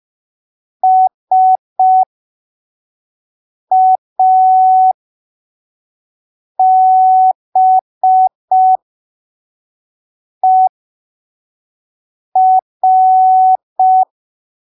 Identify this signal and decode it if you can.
Morse code, heard as a keyed tone, about 5 wpm, spaced as for 3 wpm, 750 Hz. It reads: SABER
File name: arquivos / texto / texto 21 05ppm